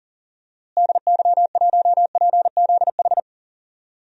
Morse code, heard as a keyed tone, about 38 wpm, 700 Hz. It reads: DY1PZH
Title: DXCC-100 Call Signs - 6 Characters - Repeated 3x - 40wpm